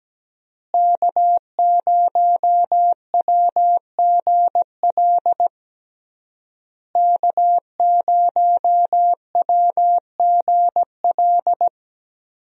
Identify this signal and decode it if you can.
Morse code, heard as a keyed tone, about 17 wpm, 700 Hz. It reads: K0WGL K0WGL
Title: USA Call Signs - 5 Characters - Repeated 3x - 17wpm